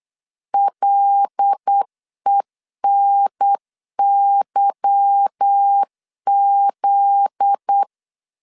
Morse code, heard as a keyed tone, about 8 wpm, 800 Hz. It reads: LENYZ
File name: LEZIONI / 8 YJ45 / 8 Y 8wpm 3245